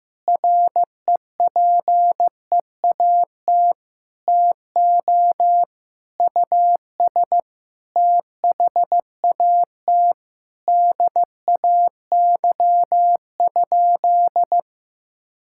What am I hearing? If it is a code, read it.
Morse code, heard as a keyed tone, about 16 wpm, 700 Hz. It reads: REPEAT TO US THAT DAY?
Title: Sentences from Top 1600 Words 15wpm